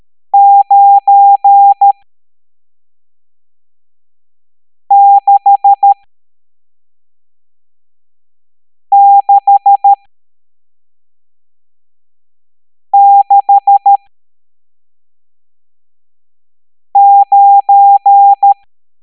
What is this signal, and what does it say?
Morse code, heard as a keyed tone, about 13 wpm, spaced as for 3 wpm, 800 Hz. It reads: 96669
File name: hamdata / L11-EX01-67890